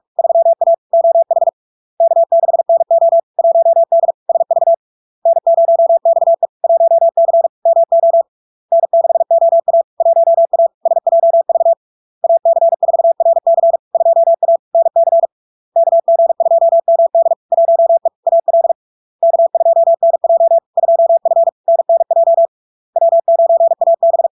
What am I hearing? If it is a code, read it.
Morse code, heard as a keyed tone, about 46 wpm, 670 Hz. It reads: 3A OH K6NO 1D SV N0XE 1C MO N6OA 1A SJV AC4RC 2A NC KG2MD 1E AL K2NJ 2F NNJ W9AB
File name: LARK-FD-2022-Log-in-CW-45wpm-670-Hz